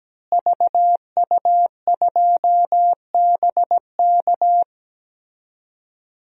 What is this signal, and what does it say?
Morse code, heard as a keyed tone, about 17 wpm, 700 Hz. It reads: VU2BK
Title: DXCC-100 Call Signs - 5 Characters - Repeated 3x - 17wpm